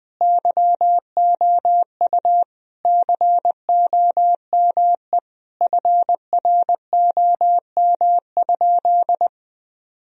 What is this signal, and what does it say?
Morse code, heard as a keyed tone, about 21 wpm, 700 Hz. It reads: YOU COME FROM?
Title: Sentences from Top 1000 Words - Review Part 2 - 20wpm